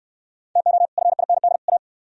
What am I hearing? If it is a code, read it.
Morse code, heard as a keyed tone, about 64 wpm, 700 Hz. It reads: TO FIND A